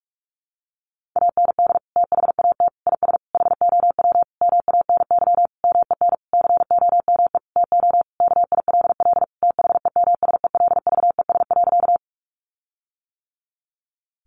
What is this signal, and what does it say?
Morse code, heard as a keyed tone, about 44 wpm, 700 Hz. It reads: AND THAT IS HOW MANY MEN COME TO KILL THEMSELVES.
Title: Sentences from Top 1400 Words 45wpm